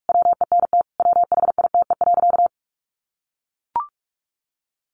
Morse code, heard as a keyed tone, about 44 wpm, 700 Hz. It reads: WENT WHITE.
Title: Sentences from Top 1200 Words 45wpm